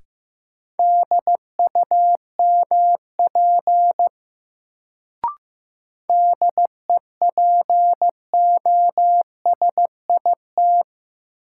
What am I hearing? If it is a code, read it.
Morse code, heard as a keyed tone, about 15 wpm, 700 Hz. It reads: DUMP DEPOSIT
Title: Sentences from Top 1300 Words - New Word Drill Alt - 15wpm